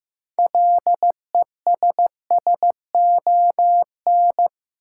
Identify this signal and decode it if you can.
Morse code, heard as a keyed tone, about 15 wpm, 700 Hz. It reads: LESSON
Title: Six Letter Words - 15wpm